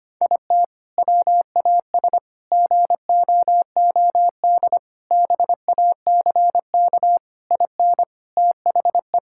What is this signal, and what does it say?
Morse code, heard as a keyed tone, about 25 wpm, 700 Hz. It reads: IT WAS GOOD BACK IN THE
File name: practice / 25wpm / Sentences with 3-Word Phrases - 25wpm